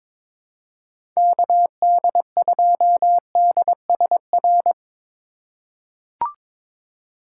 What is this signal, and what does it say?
Morse code, heard as a keyed tone, about 22 wpm, 700 Hz. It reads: KD2DSR
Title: USA Call Signs - 6 Characters - 2x Character Spacing - 22wpm